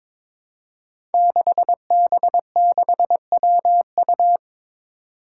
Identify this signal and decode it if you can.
Morse code, heard as a keyed tone, about 22 wpm, 700 Hz. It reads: 6B6WU
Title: Five Challenging Letter-Number Combinations 22wpm